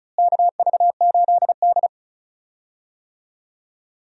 Morse code, heard as a keyed tone, about 34 wpm, 700 Hz. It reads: KV8D
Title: USA Call Signs - 4 Characters - Repeated 2x - 35wpm